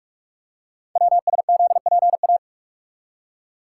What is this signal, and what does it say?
Morse code, heard as a keyed tone, about 44 wpm, 690 Hz. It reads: WSZPA
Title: Code Groups - 5 Letters - 45wpm